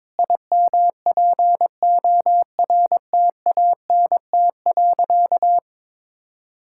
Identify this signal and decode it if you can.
Morse code, heard as a keyed tone, about 22 wpm, 700 Hz. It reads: IMPORTANT.
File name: Sentences with 7-Word Phrases - 22wpm